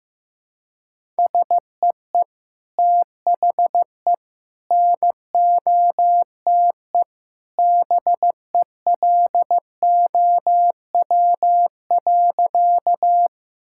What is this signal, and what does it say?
Morse code, heard as a keyed tone, about 15 wpm, 700 Hz. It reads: SEE THE NOTE BELOW.
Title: Sentences from Top 1000 Words - Review Part 5 - 15wpm